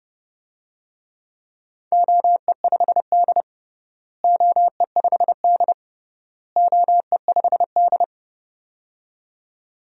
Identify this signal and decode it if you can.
Morse code, heard as a keyed tone, about 32 wpm, 700 Hz. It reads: OE5D OE5D OE5D
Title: DXCC-100 Call Signs - 4 Characters - Repeated 3x - 30wpm